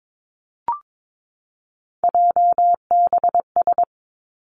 Morse code, heard as a keyed tone, about 22 wpm, 700 Hz. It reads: JBS
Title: Three Challenging Letter-Number Combinations 22wpm